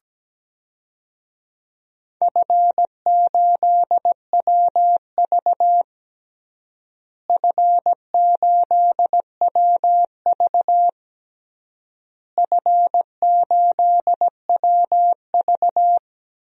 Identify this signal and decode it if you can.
Morse code, heard as a keyed tone, about 17 wpm, 700 Hz. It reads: F8WV F8WV F8WV
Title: DXCC-100 Call Signs - 4 Characters - Repeated 3x - 17wpm